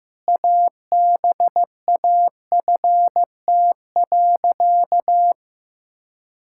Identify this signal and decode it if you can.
Morse code, heard as a keyed tone, about 15 wpm, 700 Hz. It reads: ABAFT.
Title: Pangram Challenge v2 - 15wpm